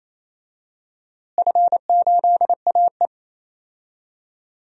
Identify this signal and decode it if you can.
Morse code, heard as a keyed tone, about 28 wpm, 700 Hz. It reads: F8AE
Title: DXCC-100 Call Signs - 4 Characters - Repeated 2x - 28wpm